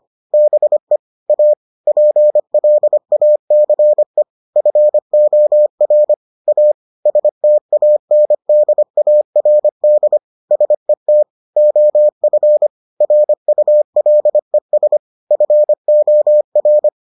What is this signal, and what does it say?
Morse code, heard as a keyed tone, about 25 wpm, 600 Hz. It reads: BE A PLACE FOR A STANDARD SET OF RULES FOR